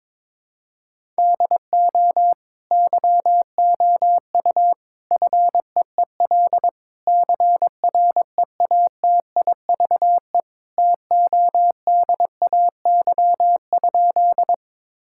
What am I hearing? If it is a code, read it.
Morse code, heard as a keyed tone, about 22 wpm, 700 Hz. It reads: DO YOU FEEL CREATIVE TODAY?
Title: Sentences from Top 1000 Words - Review Part 5 - 22wpm